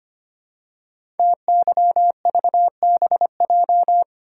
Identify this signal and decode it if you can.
Morse code, heard as a keyed tone, about 25 wpm, 700 Hz. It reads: TYVBJ
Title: Code Groups - 5 Letters - 25wpm